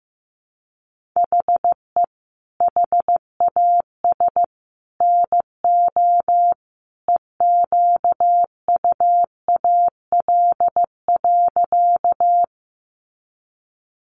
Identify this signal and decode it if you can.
Morse code, heard as a keyed tone, about 15 wpm, 700 Hz. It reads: HE HAS NO EQUAL.
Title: Sentences from Top 1400 Words 15wpm